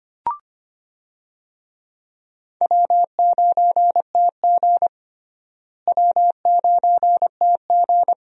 Morse code, heard as a keyed tone, about 25 wpm, 700 Hz. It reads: W9TG W9TG
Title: USA Call Signs - 4 Characters - Repeated 2x - 25wpm